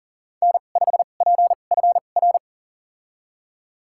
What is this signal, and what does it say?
Morse code, heard as a keyed tone, about 40 wpm, spaced as for 29 wpm, 700 Hz. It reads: N5PFR
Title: USA Call Signs - 2x Character Spacing - 40wpm